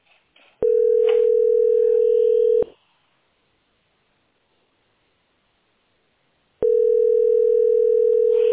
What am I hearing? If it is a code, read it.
Ringback tone (North America), 440 + 480 Hz.